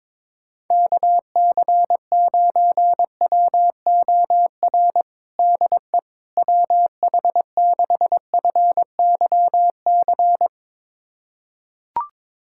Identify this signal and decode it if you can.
Morse code, heard as a keyed tone, about 22 wpm, 700 Hz. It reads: KC9WOR DE WH6FYC
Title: USA Callsign DE Callsign 22wpm